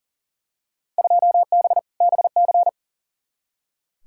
Morse code, heard as a keyed tone, about 40 wpm, 700 Hz. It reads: 2B BC
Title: Field Day Exchanges Encore - 40wpm